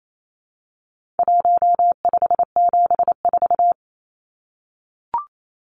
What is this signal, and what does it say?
Morse code, heard as a keyed tone, about 28 wpm, 700 Hz. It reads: 1574
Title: Numbers - 4-Digit 28wpm